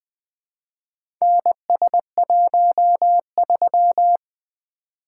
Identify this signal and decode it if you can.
Morse code, heard as a keyed tone, about 20 wpm, 700 Hz. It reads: NS13
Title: Four Letter-Number Combinations - 20wpm